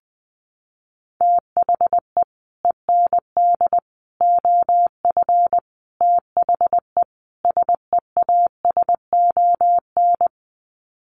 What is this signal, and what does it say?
Morse code, heard as a keyed tone, about 20 wpm, 700 Hz. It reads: THE END OF THE SEASON
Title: Sets of 5 Words Encore 20wpm